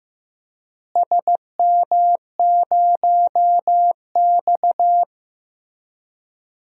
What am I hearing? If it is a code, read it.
Morse code, heard as a keyed tone, about 15 wpm, 700 Hz. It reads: SM0X
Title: DXCC-100 Call Signs - 4 Characters - Repeated 3x - 15wpm